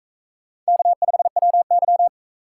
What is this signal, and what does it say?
Morse code, heard as a keyed tone, about 40 wpm, 700 Hz. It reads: K5WY